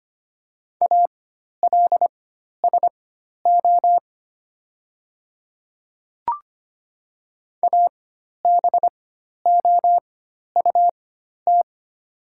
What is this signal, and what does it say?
Morse code, heard as a keyed tone, about 25 wpm, spaced as for 12 wpm, 700 Hz. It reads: ALSO ABOUT
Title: 4x Character Spacing - Top 200 Words - 25wpm